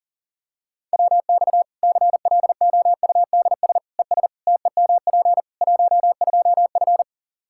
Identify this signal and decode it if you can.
Morse code, heard as a keyed tone, about 40 wpm, 700 Hz. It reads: WX CLOUDS ES TEMP 12F
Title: K1USN SST QSO Practice 40wpm